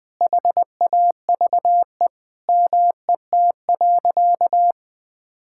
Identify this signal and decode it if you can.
Morse code, heard as a keyed tone, about 20 wpm, 700 Hz. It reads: HAVE MET.
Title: Sentences with 7-Word Phrases - 20wpm